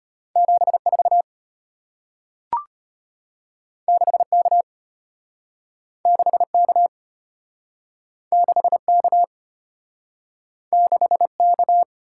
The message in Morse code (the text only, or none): 74 6K 6K 6K 6K